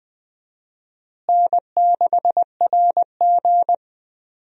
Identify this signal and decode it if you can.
Morse code, heard as a keyed tone, about 20 wpm, 700 Hz. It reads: N6RG
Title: USA Call Signs - 4 Characters - Repeated 2x - 20wpm